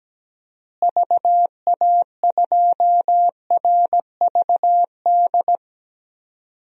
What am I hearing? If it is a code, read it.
Morse code, heard as a keyed tone, about 17 wpm, 700 Hz. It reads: VA2RVD
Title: DXCC-100 Call Signs - 6 Characters - Repeated 2x - 17wpm